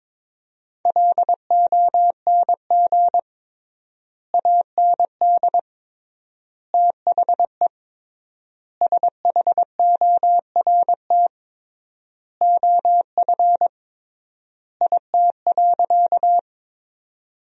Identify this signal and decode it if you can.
Morse code, heard as a keyed tone, about 22 wpm, 700 Hz. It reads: LONG AND THE SHORT OF IT.
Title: 3x Word-Spacing - Sentences from Top 400 Words - 22wpm